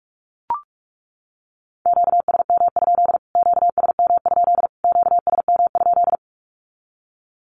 Morse code, heard as a keyed tone, about 44 wpm, 700 Hz. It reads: QSM? QSM? QSM?
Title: Q-Codes Repeated 3x 45wpm